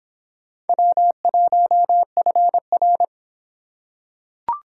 Morse code, heard as a keyed tone, about 24 wpm, 700 Hz. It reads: W1FR